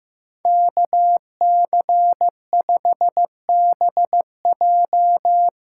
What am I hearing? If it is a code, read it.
Morse code, heard as a keyed tone, about 16 wpm, 700 Hz. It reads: KC5BJ
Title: USA Callsign DE Callsign - 5 Characters - 15wpm